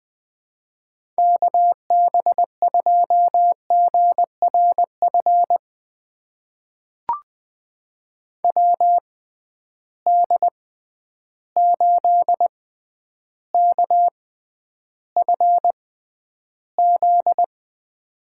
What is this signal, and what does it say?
Morse code, heard as a keyed tone, about 20 wpm, spaced as for 10 wpm, 700 Hz. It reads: KB2GRF WD8KFZ
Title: USA Call Signs - 6 Characters - 6x Character Spacing - 20wpm